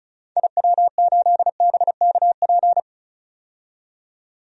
Morse code, heard as a keyed tone, about 36 wpm, 700 Hz. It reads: IW8BKP
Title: DXCC-100 Call Signs - 6 Characters - 35wpm